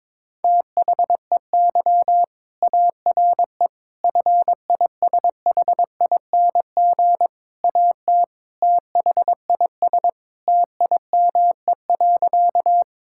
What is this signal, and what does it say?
Morse code, heard as a keyed tone, about 22 wpm, 700 Hz. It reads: THEY ARE FISHING AT THIS TIME.
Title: Sentences from Top 1000 Words - Review Part 5 - 22wpm